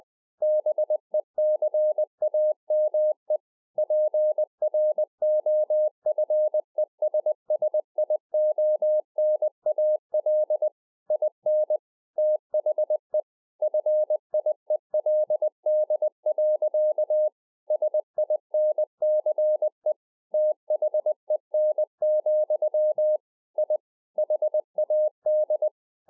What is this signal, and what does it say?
Morse code, heard as a keyed tone, about 20 wpm, 610 Hz. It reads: BECAME PROFESSIONAL IN THE FIELD. SINCE THEN, I HAD